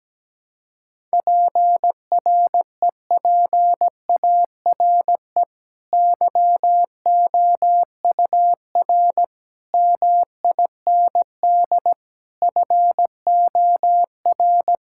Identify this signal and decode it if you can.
Morse code, heard as a keyed tone, about 17 wpm, 700 Hz. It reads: PREPARE YOUR MIND FOR
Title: Sentences from Top 1000 Words - Review Part 6 - 17wpm